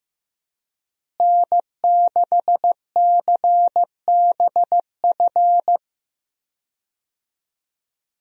Morse code, heard as a keyed tone, about 15 wpm, 700 Hz. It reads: N6CBF
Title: USA Call Signs - 5 Characters - Repeated 2x - 15wpm